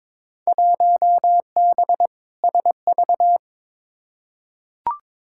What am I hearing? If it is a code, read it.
Morse code, heard as a keyed tone, about 22 wpm, 700 Hz. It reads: 1B SV
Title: Field Day Exchanges Encore - 22wpm